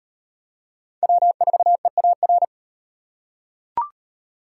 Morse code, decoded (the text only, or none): W4EAR